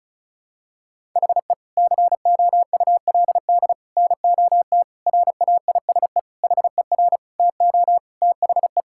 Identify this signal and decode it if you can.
Morse code, heard as a keyed tone, about 34 wpm, 700 Hz. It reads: HE COULD NOT RAISE HER TO THE
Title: Sentences from Top 1600 Words 35wpm